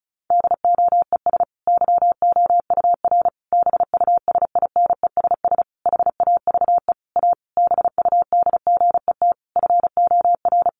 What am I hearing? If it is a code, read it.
Morse code, heard as a keyed tone, about 34 wpm, 700 Hz. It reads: DOES YOUR BUSINESS HAVE A BUDGET FOR
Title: Sentences from Top 1200 Words 35wpm